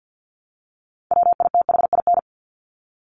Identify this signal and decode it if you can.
Morse code, heard as a keyed tone, about 50 wpm, 700 Hz. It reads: WITHIN
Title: Top 501-750 Words 50wpm